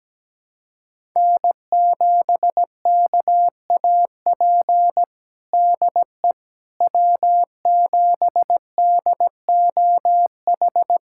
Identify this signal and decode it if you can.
Morse code, heard as a keyed tone, about 17 wpm, 700 Hz. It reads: N7KAP DE W7DOH
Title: USA Callsign DE Callsign - 5 Characters - 17wpm